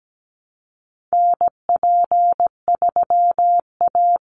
Morse code, heard as a keyed tone, about 17 wpm, 700 Hz. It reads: NP3A
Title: Call Signs Repeated 3x - 17wpm